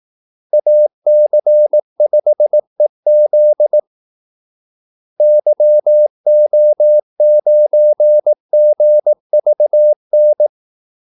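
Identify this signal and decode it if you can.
Morse code, heard as a keyed tone, about 18 wpm, 600 Hz. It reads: AC5EZ YO9GVN
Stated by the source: Morse call-sign practice at 18 wpm.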